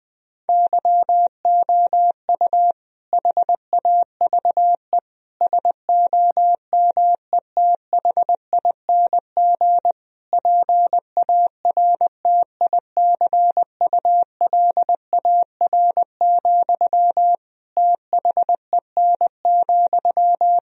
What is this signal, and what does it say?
Morse code, heard as a keyed tone, about 20 wpm, 700 Hz. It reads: YOU HAVE SOMETHING PARTICULAR, THEN,
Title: Sentences from Top 1000 Words - Review Part 4 - 20wpm